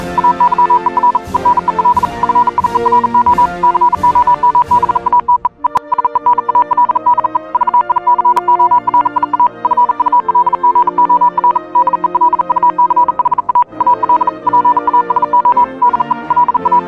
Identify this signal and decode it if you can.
Morse code, heard as a keyed tone, about 40 wpm, 980 Hz, with noise. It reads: AYER EN EL FIERO COMBATE EE SENAL DE VICORIA FULGOR DE LUCHA AL PASO DE LA
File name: morse hymne(1)